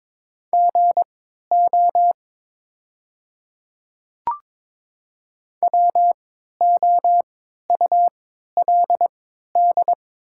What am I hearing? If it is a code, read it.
Morse code, heard as a keyed tone, about 22 wpm, spaced as for 12 wpm, 690 Hz. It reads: GO WOULD
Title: 3x Character Spacing - Top 100 Words - 22wpm